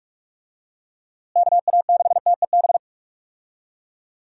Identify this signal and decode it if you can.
Morse code, heard as a keyed tone, about 47 wpm, 700 Hz. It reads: KA6TEB
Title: USA Call Signs - 6 Characters - Repeated 3x - 45wpm